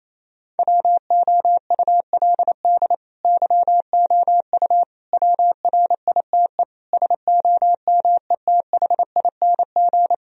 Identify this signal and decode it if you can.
Morse code, heard as a keyed tone, about 28 wpm, 700 Hz. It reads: WOULD YOU WRITE SOMETHING
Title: Sentences from Top 1000 Words - Review Part 2 - 28wpm